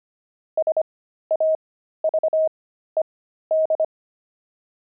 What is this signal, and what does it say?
Morse code, heard as a keyed tone, about 25 wpm, spaced as for 13 wpm, 630 Hz. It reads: SAVED
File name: pangram-fs-13-cs-25-axle